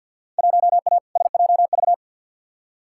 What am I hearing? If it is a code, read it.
Morse code, heard as a keyed tone, about 46 wpm, 700 Hz. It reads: 1A SJV
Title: Field Day Exchanges Encore - 50wpm